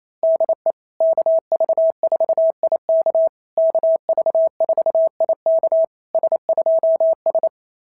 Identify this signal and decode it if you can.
Morse code, heard as a keyed tone, about 27 wpm, 660 Hz. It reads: DE KV4IK KV4IK S2S
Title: SOTA QSO Practice 28wpm